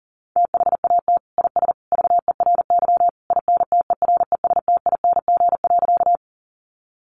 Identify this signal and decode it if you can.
Morse code, heard as a keyed tone, about 40 wpm, 700 Hz. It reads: THAT IS VERY INTERESTING.
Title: Sentences from Top 1000 Words 40wpm